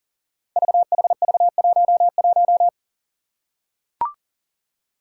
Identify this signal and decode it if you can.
Morse code, heard as a keyed tone, about 38 wpm, 700 Hz. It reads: VHV11
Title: Five Challenging Letter-Number Combinations 40wpm